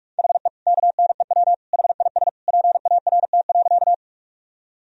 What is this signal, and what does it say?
Morse code, heard as a keyed tone, about 46 wpm, 700 Hz. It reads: HE KNEW HIS PART.
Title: Sentences from Top 1000 Words - Review Part 1 - 45wpm